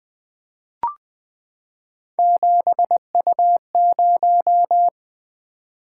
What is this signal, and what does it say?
Morse code, heard as a keyed tone, about 20 wpm, 700 Hz. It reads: 7U0
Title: Three Letter-Number Combinations - 20wpm